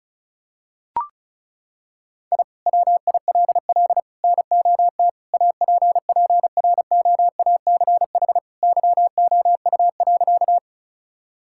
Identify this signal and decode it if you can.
Morse code, heard as a keyed tone, about 36 wpm, 700 Hz. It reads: I WILL NOT APPROACH YOU.
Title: Sentences from Top 1000 Words - Review Part 6 - 35wpm